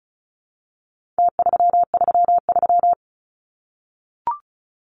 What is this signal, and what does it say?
Morse code, heard as a keyed tone, about 34 wpm, 700 Hz. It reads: T333
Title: Numbers - Contest Style - 4-Digit 35wpm